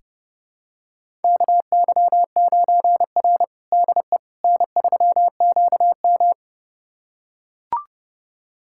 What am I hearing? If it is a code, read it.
Morse code, heard as a keyed tone, about 30 wpm, 700 Hz. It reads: KY9R DE N3QM
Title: USA Callsign DE Callsign - 4 Characters - 30wpm